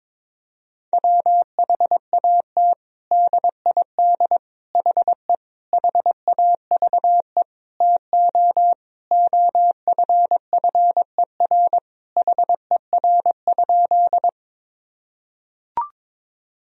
Morse code, heard as a keyed tone, about 22 wpm, 700 Hz. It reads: WHAT DID HE HAVE TO OFFER HER?
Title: Sentences from Top 1000 Words - Review Part 4 - 22wpm